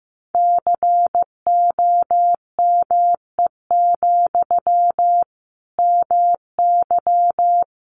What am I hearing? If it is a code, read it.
Morse code, heard as a keyed tone, about 14 wpm, 700 Hz. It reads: COME, MY
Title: Sentences from Top 800 Words 15wpm